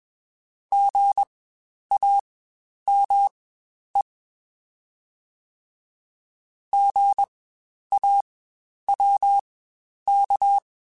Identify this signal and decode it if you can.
Morse code, heard as a keyed tone, about 21 wpm, spaced as for 10 wpm, 800 Hz. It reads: GAME GAWK
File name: Spaced Alphas / Spaced Alphas mp3s / Gs sp @20